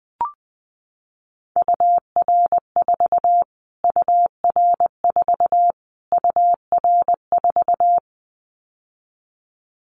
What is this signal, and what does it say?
Morse code, heard as a keyed tone, about 20 wpm, 700 Hz. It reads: UR4 UR4 UR4
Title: Callsign Prefixes for Top 25 DXCC - Repeated 3x - 20wpm